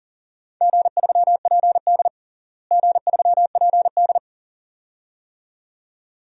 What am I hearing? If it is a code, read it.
Morse code, heard as a keyed tone, about 40 wpm, 700 Hz. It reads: G3PD G3PD
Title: DXCC-100 Call Signs - 4 Characters - Repeated 2x - 40wpm